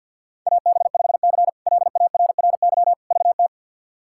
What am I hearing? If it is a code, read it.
Morse code, heard as a keyed tone, about 52 wpm, 700 Hz. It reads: AB5C LARRY VT
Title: K1USN SST QSO Practice - 50wpm